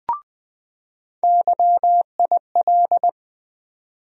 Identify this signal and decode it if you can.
Morse code, heard as a keyed tone, about 20 wpm, 700 Hz. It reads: YIL
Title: USA Call Sign Postfixes - 3 Letters 20wpm